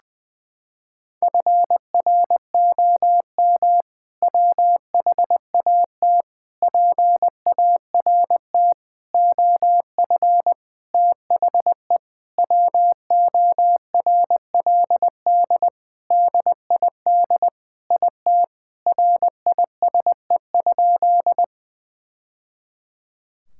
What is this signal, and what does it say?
Morse code, heard as a keyed tone, about 20 wpm, 700 Hz. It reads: FROM WHAT PART OF THE WORLD DID IT RISE?
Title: Sentences from Top 1000 Words - Review Part 6 - 20wpm